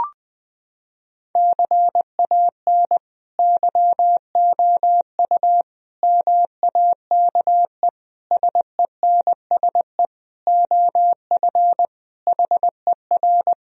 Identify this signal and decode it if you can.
Morse code, heard as a keyed tone, about 21 wpm, 700 Hz. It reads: CAN YOU MAKE SENSE OF HER
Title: Sentences from Top 1000 Words - Review Part 4 - 20wpm